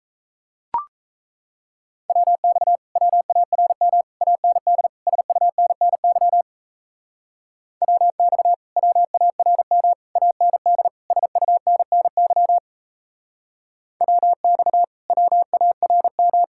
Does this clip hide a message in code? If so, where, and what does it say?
Yes: Morse code from 2.1 to 16.4 s, heard as a keyed tone, about 37 wpm, 700 Hz. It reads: WX WARM AND SUNNY WX WARM AND SUNNY WX WARM